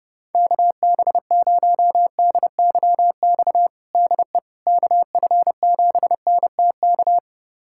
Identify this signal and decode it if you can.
Morse code, heard as a keyed tone, about 30 wpm, 690 Hz. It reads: KB0DYX DE KF7NTK
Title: USA Callsign DE Callsign - 6 Characters - 30wpm